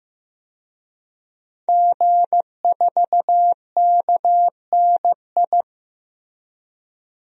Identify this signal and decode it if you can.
Morse code, heard as a keyed tone, about 15 wpm, 700 Hz. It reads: G4KNI
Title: DXCC-100 Call Signs - 5 Characters - 15wpm